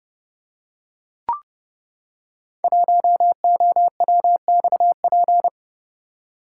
Morse code, heard as a keyed tone, about 30 wpm, 700 Hz. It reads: 1OWXP
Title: Code Groups - 5 Letter:Numbers - 30wpm